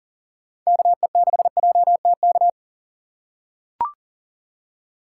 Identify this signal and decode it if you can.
Morse code, heard as a keyed tone, about 40 wpm, 700 Hz. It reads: KE6JTK
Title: USA Call Signs - 6x Character Spacing - 40wpm